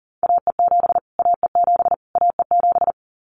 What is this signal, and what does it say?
Morse code, heard as a keyed tone, about 40 wpm, 700 Hz. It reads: AE7 AE7 AE7
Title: Callsign Prefixes for Top 25 DXCC - Repeated 3x - 40wpm